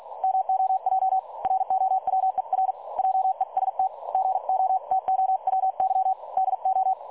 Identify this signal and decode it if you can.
Morse code, heard as a keyed tone, about 48 wpm, 760 Hz. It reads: MOJ ROWER JEST GOTOWY DO
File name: dlazaawansowanych / nagraniazpasma / websdr recording 3565.0kHz